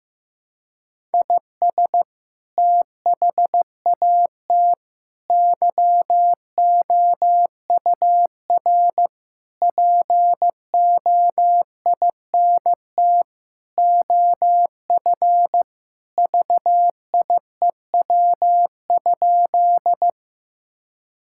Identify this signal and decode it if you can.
Morse code, heard as a keyed tone, about 15 wpm, 700 Hz. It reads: IS THAT YOUR POINT OF VIEW?
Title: Sentences from Top 1000 Words - Review Part 6 - 15wpm